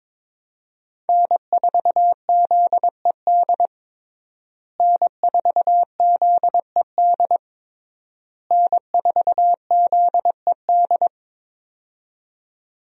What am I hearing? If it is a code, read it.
Morse code, heard as a keyed tone, about 22 wpm, 700 Hz. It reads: N4ZED N4ZED N4ZED
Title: USA Call Signs - 5 Characters - Repeated 3x - 22wpm